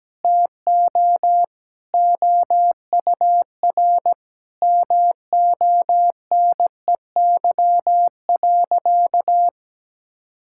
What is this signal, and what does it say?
Morse code, heard as a keyed tone, about 17 wpm, 700 Hz. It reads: TO OUR MONEY.
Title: Sentences from Top 1600 Words 17wpm